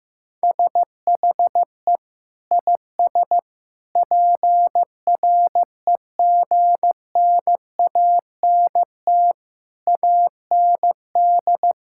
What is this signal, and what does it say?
Morse code, heard as a keyed tone, about 15 wpm, 700 Hz. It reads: SHE IS PREGNANT AND
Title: Sentences from Top 1000 Words - Review Part 6 - 15wpm